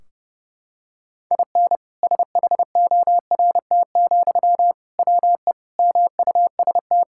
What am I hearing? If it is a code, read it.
Morse code, heard as a keyed tone, about 30 wpm, 700 Hz. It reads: IN SHORT, WE MUST
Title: Sentences from Top 1600 Words 30wpm